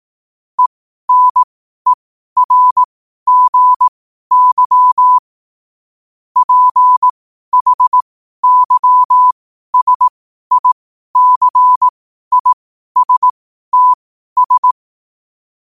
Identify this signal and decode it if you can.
Morse code, heard as a keyed tone, about 18 wpm, spaced as for 12 wpm, 1000 Hz. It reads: ENERGY PHYSICISTS
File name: qotdfiles / 12wpm